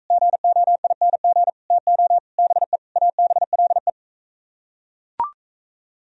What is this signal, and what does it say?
Morse code, heard as a keyed tone, about 41 wpm, 700 Hz. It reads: GOING TO BE ABLE